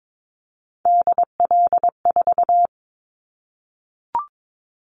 Morse code, heard as a keyed tone, about 22 wpm, 700 Hz. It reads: DL4
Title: Callsign Prefixes for Top 50 DXCC - 22wpm